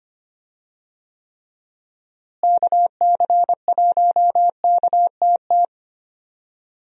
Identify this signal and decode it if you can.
Morse code, heard as a keyed tone, about 25 wpm, 700 Hz. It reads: KC1KTT
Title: USA Call Signs - 6 Characters - Repeated 3x - 25wpm